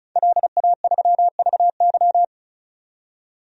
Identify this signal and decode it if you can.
Morse code, heard as a keyed tone, about 36 wpm, 700 Hz. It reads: LA3VY
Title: DXCC-100 Call Signs - 5 Characters - 35wpm